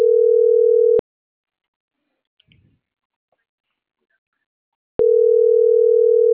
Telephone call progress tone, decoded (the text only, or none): ringback tone